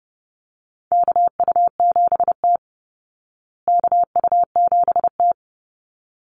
Morse code, heard as a keyed tone, about 30 wpm, 700 Hz. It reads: KU7T KU7T
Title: Call Signs Repeated 3x - 30wpm